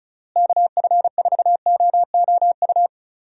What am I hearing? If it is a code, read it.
Morse code, heard as a keyed tone, about 34 wpm, 700 Hz. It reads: KF4OOU
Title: USA Call Signs - 6 Characters - Repeated 2x - 35wpm